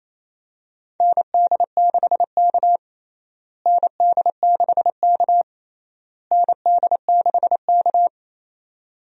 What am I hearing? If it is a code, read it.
Morse code, heard as a keyed tone, about 28 wpm, 700 Hz. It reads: ND6K ND6K ND6K
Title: USA Call Signs - 4 Characters - Repeated 3x - 28wpm